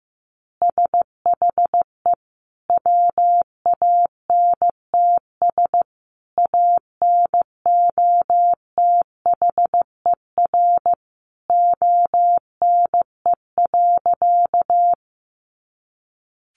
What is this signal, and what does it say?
Morse code, heard as a keyed tone, about 15 wpm, 700 Hz. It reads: SHE WANTS ANOTHER ONE.
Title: Easy Phrases 15wpm